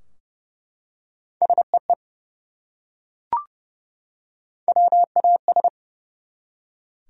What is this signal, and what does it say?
Morse code, heard as a keyed tone, about 30 wpm, 700 Hz. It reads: SEE WAS
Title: Top 100 Words 30wpm